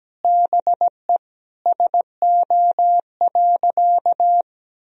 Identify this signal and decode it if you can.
Morse code, heard as a keyed tone, about 17 wpm, 700 Hz. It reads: BE SO.
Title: Sentences from Top 1000 Words - Review Part 1 - 17wpm